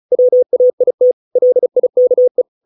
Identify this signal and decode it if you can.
Morse code, heard as a keyed tone, about 34 wpm, 500 Hz. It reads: WAIT LIKE